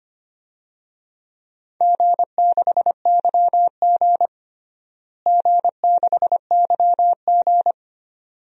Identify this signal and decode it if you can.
Morse code, heard as a keyed tone, about 25 wpm, 700 Hz. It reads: G6YG G6YG
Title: DXCC-100 Call Signs - 4 Characters - Repeated 2x - 25wpm